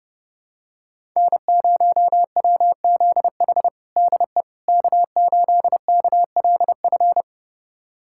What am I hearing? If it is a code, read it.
Morse code, heard as a keyed tone, about 30 wpm, 700 Hz. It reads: N0WZH DE K8KLF
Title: USA Callsign DE Callsign - 5 Characters - 30wpm